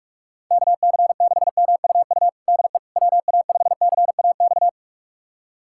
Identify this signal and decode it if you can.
Morse code, heard as a keyed tone, about 41 wpm, 700 Hz. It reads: KC6MUU DE WA5CAX
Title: USA Callsign DE Callsign - 6 Characters - 45wpm